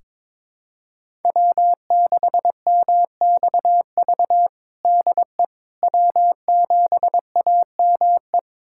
Morse code, heard as a keyed tone, about 22 wpm, 700 Hz. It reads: W6MXV DE W7AME
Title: USA Callsign DE Callsign - 5 Characters - 22wpm